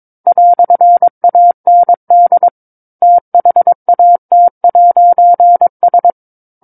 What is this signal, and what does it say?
Morse code, heard as a keyed tone, about 22 wpm, 700 Hz. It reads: "AND THAT'S
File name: 06-Chapter-06 22 WPM